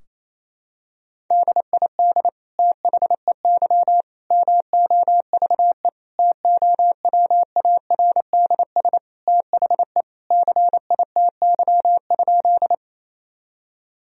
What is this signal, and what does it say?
Morse code, heard as a keyed tone, about 28 wpm, 700 Hz. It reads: DID THEY MOVE TOWARDS THE CITY?
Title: Sentences from Top 1000 Words - Review Part 2 - 28wpm